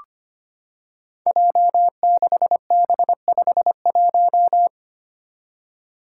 Morse code, heard as a keyed tone, about 25 wpm, 700 Hz. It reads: J6B51
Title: Five Challenging Letter-Number Combinations 25wpm